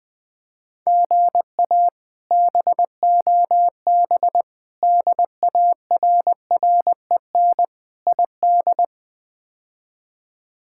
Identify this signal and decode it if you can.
Morse code, heard as a keyed tone, about 20 wpm, 700 Hz. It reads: GA BOB DARREN ID
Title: K1USN SST QSO Practice - 20wpm